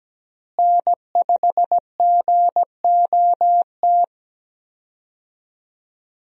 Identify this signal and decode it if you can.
Morse code, heard as a keyed tone, about 17 wpm, 700 Hz. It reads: N5GOT
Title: USA Call Signs - 5 Characters - Repeated 2x - 17wpm